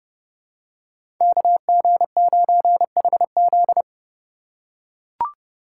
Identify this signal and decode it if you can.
Morse code, heard as a keyed tone, about 30 wpm, 700 Hz. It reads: KG9HZ
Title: USA Call Signs - 5 Characters - 2x Character Spacing - 30wpm